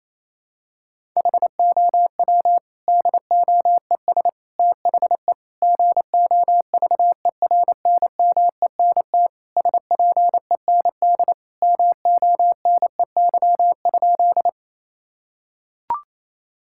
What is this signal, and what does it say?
Morse code, heard as a keyed tone, about 28 wpm, 700 Hz. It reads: HOW DOES THE GOVERNMENT SPEND MONEY?
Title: Sentences from Top 1000 Words - Review Part 5 - 28wpm